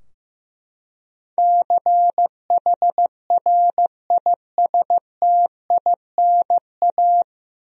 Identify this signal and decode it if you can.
Morse code, heard as a keyed tone, about 15 wpm, 700 Hz. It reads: CHRISTINA
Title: USA Common Names - 15wpm